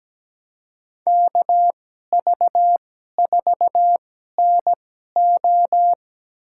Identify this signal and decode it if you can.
Morse code, heard as a keyed tone, about 17 wpm, spaced as for 12 wpm, 700 Hz. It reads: KV4NO
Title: USA Call Signs - 5 Characters - 2x Character Spacing - 17wpm